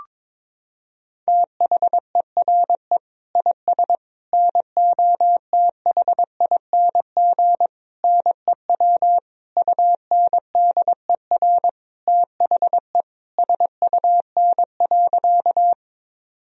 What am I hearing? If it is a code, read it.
Morse code, heard as a keyed tone, about 22 wpm, 700 Hz. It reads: THERE IS NOTHING NEW UNDER THE SUN.
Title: Sentences from Top 1000 Words - Review Part 1 - 22wpm